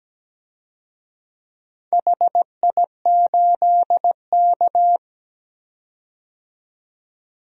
Morse code, heard as a keyed tone, about 17 wpm, 700 Hz. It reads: HI8K
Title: DXCC-100 Call Signs - 4 Characters - 17wpm